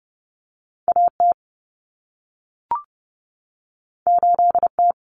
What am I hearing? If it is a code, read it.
Morse code, heard as a keyed tone, about 30 wpm, 700 Hz. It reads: AT 8T
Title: Common Cut Numbers - 2-Digit 30wpm